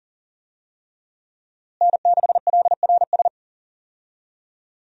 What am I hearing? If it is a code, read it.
Morse code, heard as a keyed tone, about 40 wpm, 700 Hz. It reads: N6LRS
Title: USA Call Signs - 5 Characters - Repeated 3x - 40wpm